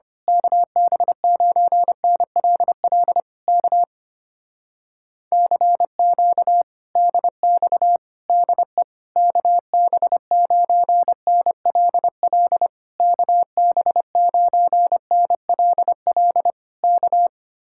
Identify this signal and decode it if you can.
Morse code, heard as a keyed tone, about 25 wpm, 700 Hz. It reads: KB9NLL K CQ DX DE KB9NLL KB9NLL K
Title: Speed-Racing Common QSO Phrases Encore - 25wpm